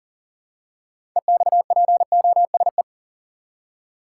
Morse code, heard as a keyed tone, about 40 wpm, 700 Hz. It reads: EXPOSE
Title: Six Letter Words - 40wpm